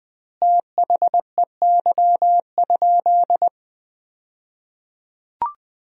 Morse code, heard as a keyed tone, about 20 wpm, 700 Hz. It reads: THEY?
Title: 2x Word-Spacing - Sentences from Top 300 Words - 20wpm